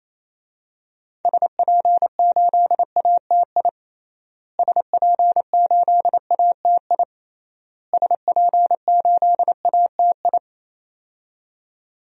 Morse code, heard as a keyed tone, about 28 wpm, 700 Hz. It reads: SP8ATI SP8ATI SP8ATI
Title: DXCC-100 Call Signs - 6 Characters - Repeated 3x - 28wpm